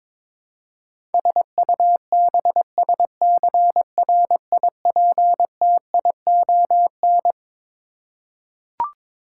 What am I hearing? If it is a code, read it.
Morse code, heard as a keyed tone, about 21 wpm, 700 Hz. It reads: SUBSCRIPTION
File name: Twelve Letter Words - 22wpm